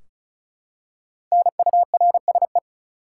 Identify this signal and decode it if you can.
Morse code, heard as a keyed tone, about 36 wpm, 700 Hz. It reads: NURSE
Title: Five Letter Words - 35wpm